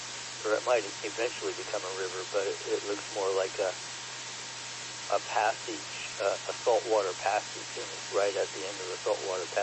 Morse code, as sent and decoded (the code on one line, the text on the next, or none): none